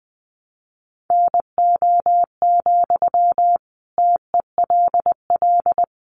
Morse code, heard as a keyed tone, about 20 wpm, 700 Hz. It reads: NO, TELL
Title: Sentences from Top 200 Words 20wpm